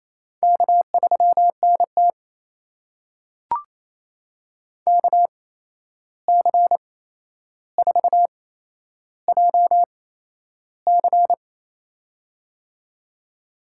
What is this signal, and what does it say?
Morse code, heard as a keyed tone, about 28 wpm, spaced as for 11 wpm, 700 Hz. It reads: K3NT KC4JC
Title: USA Call Signs - 8x Character Spacing - 28wpm